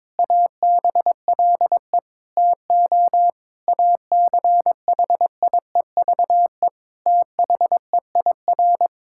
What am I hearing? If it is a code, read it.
Morse code, heard as a keyed tone, about 22 wpm, 700 Hz. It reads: ABLE TO ACHIEVE THEIR
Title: Sentences from Top 1000 Words - Review Part 6 - 22wpm